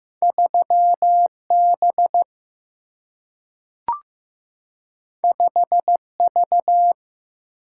Morse code, heard as a keyed tone, about 15 wpm, 700 Hz. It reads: 3 B 5 V
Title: Single-Letter Single-Number 15wpm